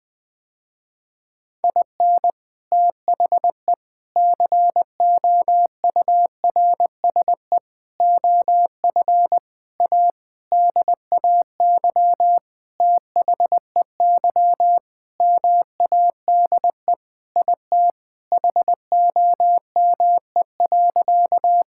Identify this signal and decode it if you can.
Morse code, heard as a keyed tone, about 21 wpm, 700 Hz. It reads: IN THE COURSE OF A DAY THEY MADE IT HOME.
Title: Sentences with 4-Word Phrases - Spoken Beginning - 20wpm